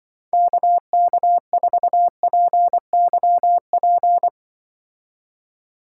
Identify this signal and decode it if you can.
Morse code, heard as a keyed tone, about 25 wpm, 700 Hz. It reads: KK4PYP
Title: Speed Racing - USA Call Signs 20wpm